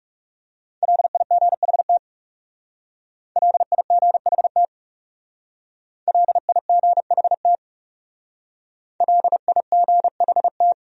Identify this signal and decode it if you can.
Morse code, heard as a keyed tone, about 37 wpm, 700 Hz. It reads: LIGHT LIGHT LIGHT LIGHT
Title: Speed-Racing Top 200 Words 30wpm